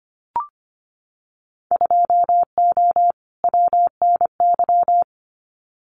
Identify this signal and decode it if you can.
Morse code, heard as a keyed tone, about 25 wpm, 700 Hz. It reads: 2O WNY
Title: Winter Field Day Exchanges Encore 25wpm